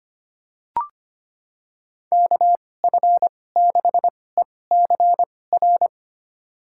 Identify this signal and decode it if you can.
Morse code, heard as a keyed tone, about 25 wpm, spaced as for 18 wpm, 700 Hz. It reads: KF6ECR
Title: USA Call Signs - 6 Characters - 2x Character Spacing - 25wpm